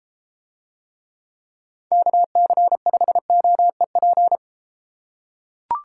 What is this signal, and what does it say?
Morse code, heard as a keyed tone, about 33 wpm, 700 Hz. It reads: KC5OEP